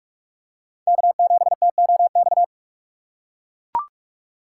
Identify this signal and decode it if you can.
Morse code, heard as a keyed tone, about 42 wpm, 700 Hz. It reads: K7TOX